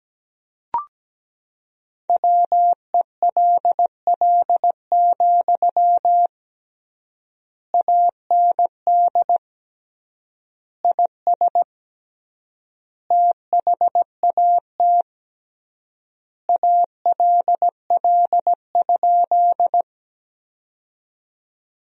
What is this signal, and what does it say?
Morse code, heard as a keyed tone, about 17 wpm, 700 Hz. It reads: WELL, AND IS THAT ALL?
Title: 3x Word-Spacing - Sentences from Top 200 Words - 17wpm